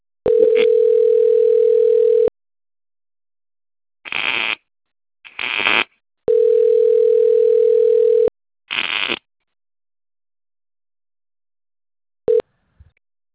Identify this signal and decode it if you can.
Ringback tone (North America), 440 + 480 Hz.